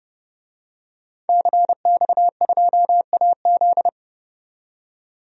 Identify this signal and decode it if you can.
Morse code, heard as a keyed tone, about 32 wpm, 700 Hz. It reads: CX2AZ
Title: DXCC-100 Call Signs - 5 Characters - 30wpm